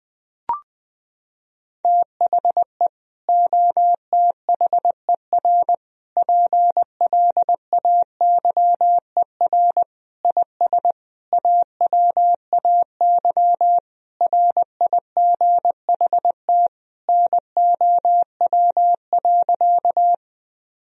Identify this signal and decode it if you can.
Morse code, heard as a keyed tone, about 20 wpm, 700 Hz. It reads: THE OTHER PLAYER IS AWAY RIGHT NOW.
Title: Sentences from Top 1000 Words - Review Part 4 - 20wpm